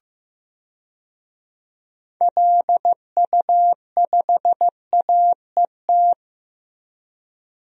Morse code, heard as a keyed tone, about 15 wpm, 700 Hz. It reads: LU5AET
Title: DXCC-100 Call Signs - 6 Characters - Repeated 2x - 15wpm